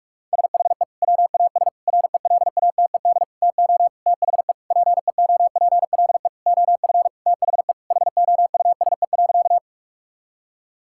Morse code, heard as a keyed tone, about 46 wpm, 700 Hz. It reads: SHE WAS RELATED TO THE PEOPLE OF THE HOUSE.
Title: Sentences from Top 1000 Words - Review Part 5 - 45wpm